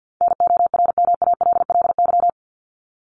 Morse code, heard as a keyed tone, about 50 wpm, 700 Hz. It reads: NORMALLY
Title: LY Suffix - Top 100 Words with Base Words 50wpm